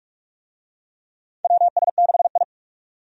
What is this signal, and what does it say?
Morse code, heard as a keyed tone, about 42 wpm, 700 Hz. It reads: WS6I
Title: USA Call Signs - 4 Characters - 45wpm